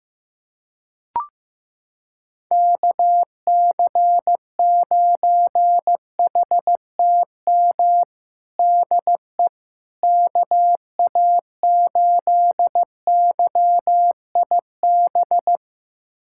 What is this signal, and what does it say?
Morse code, heard as a keyed tone, about 15 wpm, 700 Hz. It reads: KC9HTM DE KA8YIB
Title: USA Callsign DE Callsign 15wpm